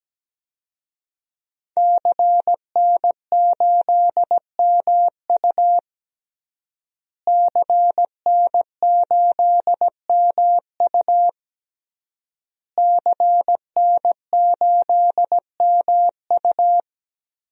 Morse code, heard as a keyed tone, about 17 wpm, 700 Hz. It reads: CN8MU CN8MU CN8MU
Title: DXCC-100 Call Signs - 5 Characters - Repeated 3x - 17wpm